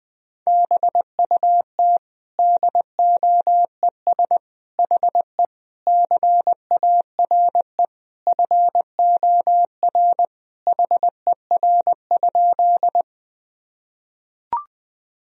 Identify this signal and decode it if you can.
Morse code, heard as a keyed tone, about 20 wpm, 700 Hz. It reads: BUT DOES HE CARE FOR HER?
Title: Sentences from Top 1000 Words - Review Part 1 - 20wpm